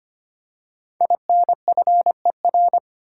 Morse code, heard as a keyed tone, about 25 wpm, 700 Hz. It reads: INFER
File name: 5 Letter Words - First 3 Letters Given - 25wpm